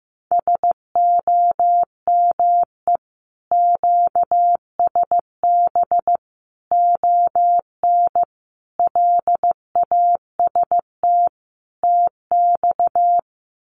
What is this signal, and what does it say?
Morse code, heard as a keyed tone, about 15 wpm, 700 Hz. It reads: SOME QSB ON LAST TX
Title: Common QSO Phrases 15wpm